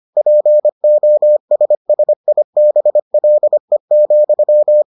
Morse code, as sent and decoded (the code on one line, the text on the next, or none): .--. --- ... ... .. -... .-.. --..--
POSSIBLE,